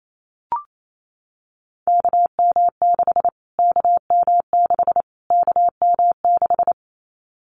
Morse code, heard as a keyed tone, about 28 wpm, 700 Hz. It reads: KM6 KM6 KM6
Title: Callsign Prefixes for Top 25 DXCC - Repeated 3x - 28wpm